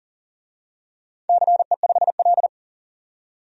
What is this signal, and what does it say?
Morse code, heard as a keyed tone, about 40 wpm, 700 Hz. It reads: CE5L
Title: DXCC-100 Call Signs - 4 Characters - 40wpm